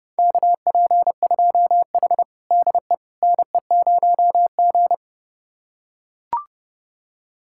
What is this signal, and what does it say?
Morse code, heard as a keyed tone, about 30 wpm, 700 Hz. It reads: KP2H DE NE0G
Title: USA Callsign DE Callsign - 4 Characters - 30wpm